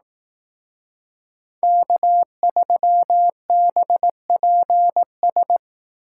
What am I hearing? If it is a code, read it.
Morse code, heard as a keyed tone, about 18 wpm, 700 Hz. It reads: K3BPS
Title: Speed Racing - USA 1x3 Call Signs 15wpm